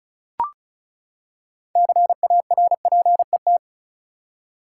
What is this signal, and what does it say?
Morse code, heard as a keyed tone, about 36 wpm, 700 Hz. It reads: CARPET
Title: Six Letter Words - 35wpm